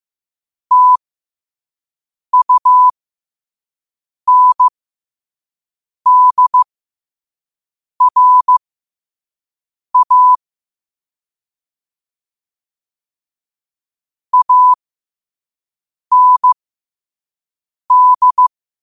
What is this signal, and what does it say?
Morse code, heard as a keyed tone, about 15 wpm, spaced as for 5 wpm, 1000 Hz. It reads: TUNDRA AND
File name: qotdfiles / 5wpm